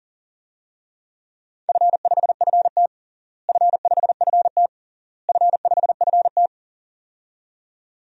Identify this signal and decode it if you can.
Morse code, heard as a keyed tone, about 40 wpm, 700 Hz. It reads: F5FT F5FT F5FT
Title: DXCC-100 Call Signs - 4 Characters - Repeated 3x - 40wpm